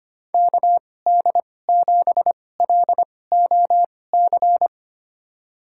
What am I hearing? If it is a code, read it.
Morse code, heard as a keyed tone, about 25 wpm, spaced as for 18 wpm, 700 Hz. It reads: KD7LOC
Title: USA Call Signs - 6 Characters - 2x Character Spacing - 25wpm